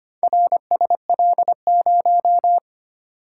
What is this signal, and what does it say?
Morse code, heard as a keyed tone, about 25 wpm, 690 Hz. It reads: RSL0
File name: practice / 25wpm / Four Letter-Number Combinations - 25wpm